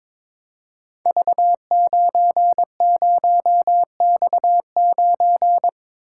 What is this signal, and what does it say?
Morse code, heard as a keyed tone, about 22 wpm, 700 Hz. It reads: V90X9
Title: Code Groups - 5 Letter:Numbers - 22wpm